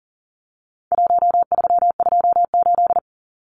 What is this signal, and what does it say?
Morse code, heard as a keyed tone, about 40 wpm, 700 Hz. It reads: 1328
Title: Numbers - 4-Digit 40wpm